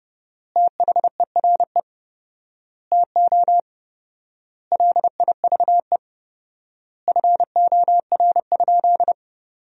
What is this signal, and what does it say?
Morse code, heard as a keyed tone, about 32 wpm, 700 Hz. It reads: THERE TO LIVE FOR?
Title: 4x Word-Spacing - Sentences from Top 200 Words - 30wpm